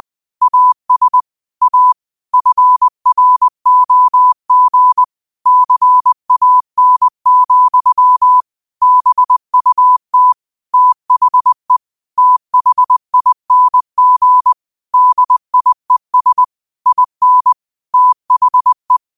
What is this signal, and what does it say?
Morse code, heard as a keyed tone, about 20 wpm, 1000 Hz. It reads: AS A FROG CAN, BUT THE THING DIES IN THE
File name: qotdfiles / 20wpm